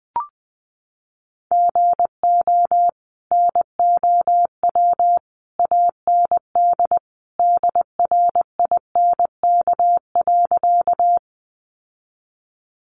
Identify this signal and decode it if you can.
Morse code, heard as a keyed tone, about 20 wpm, 700 Hz. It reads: GO NOW AND DRINK.
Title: Sentences from Top 1500 Words 20wpm